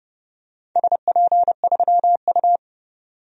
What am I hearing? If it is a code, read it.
Morse code, heard as a keyed tone, about 30 wpm, 700 Hz. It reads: SP3U
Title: DXCC-100 Call Signs - 4 Characters - 30wpm